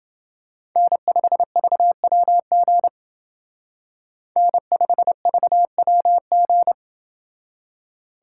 Morse code, heard as a keyed tone, about 30 wpm, 700 Hz. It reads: N5VWG N5VWG